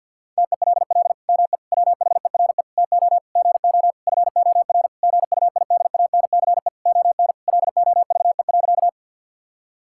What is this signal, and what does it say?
Morse code, heard as a keyed tone, about 48 wpm, 700 Hz. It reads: TELL ME WHERE TO GO FOR GUIDANCE ON LOVE.
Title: Sentences from Top 1600 Words 50wpm